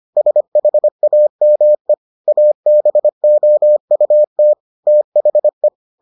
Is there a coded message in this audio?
Yes: Morse code from 0.2 to 5.7 s, heard as a keyed tone, about 25 wpm, 600 Hz. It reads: SHAME ABOUT THE